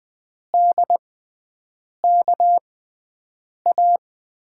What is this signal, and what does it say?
Morse code, heard as a keyed tone, about 20 wpm, spaced as for 7 wpm, 700 Hz. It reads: DKA
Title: USA Call Signs - 6 Characters - 6x Character Spacing - 20wpm